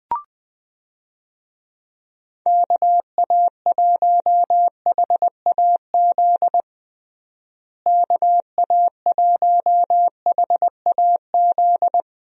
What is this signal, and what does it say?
Morse code, heard as a keyed tone, about 20 wpm, 700 Hz. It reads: KA1HAZ KA1HAZ
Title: USA Call Signs - 6 Characters - Repeated 2x - 20wpm